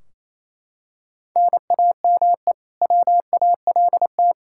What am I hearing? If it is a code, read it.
Morse code, heard as a keyed tone, about 28 wpm, 700 Hz. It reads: NAME WALT
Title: Practice QSO - Exchange 1 - Beginning - 28wpm